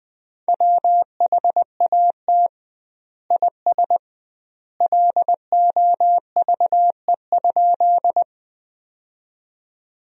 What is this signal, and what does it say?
Morse code, heard as a keyed tone, about 20 wpm, 700 Hz. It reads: WHAT IS LOVE?
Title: 2x Word-Spacing - Sentences from Top 400 Words - 20wpm